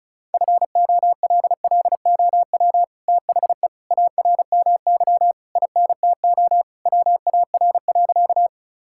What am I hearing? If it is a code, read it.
Morse code, heard as a keyed tone, about 34 wpm, 700 Hz. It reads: FOLLOW THE ARMY INTO WAR.
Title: Sentences from Top 1000 Words - Review Part 5 - 35wpm